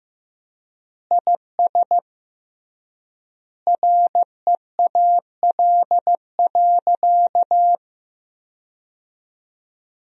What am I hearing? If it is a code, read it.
Morse code, heard as a keyed tone, about 15 wpm, 700 Hz. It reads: IS REAL.
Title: 3x Word-Spacing - Sentences from Top 300 Words - 15wpm